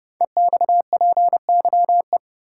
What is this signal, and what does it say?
Morse code, heard as a keyed tone, about 30 wpm, 690 Hz. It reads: EXPYE
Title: Code Groups - 5 Letters - 30wpm